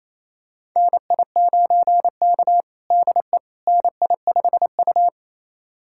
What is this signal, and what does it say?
Morse code, heard as a keyed tone, about 28 wpm, 700 Hz. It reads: NI9K DE NI5U
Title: USA Callsign DE Callsign - 4 Characters - 28wpm